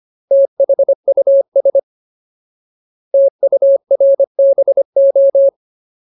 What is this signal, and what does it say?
Morse code, heard as a keyed tone, about 25 wpm, 550 Hz. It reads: THUS TURBO
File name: Ultimate Word List / 25WPM Ultimate